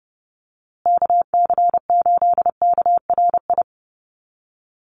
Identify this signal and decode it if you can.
Morse code, heard as a keyed tone, about 30 wpm, 700 Hz. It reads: KC8KRI
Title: USA Call Signs Repeated 3x 30wpm